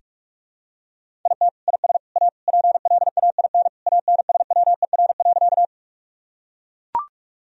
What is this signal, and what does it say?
Morse code, heard as a keyed tone, about 41 wpm, 700 Hz. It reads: IT IS A PLAIN ANSWER.
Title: Sentences from Top 1000 Words - Review Part 2 - 45wpm